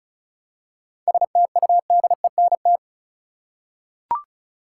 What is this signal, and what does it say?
Morse code, heard as a keyed tone, about 34 wpm, 700 Hz. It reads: STUDENT
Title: Seven Letter Words - 35wpm